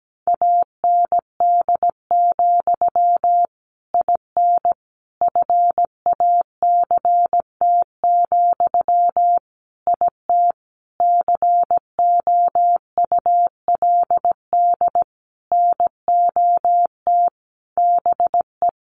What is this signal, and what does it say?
Morse code, heard as a keyed tone, about 17 wpm, 700 Hz. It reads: AND, IN FACT, IT COULD NOT BE
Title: Sentences from Top 1200 Words 17wpm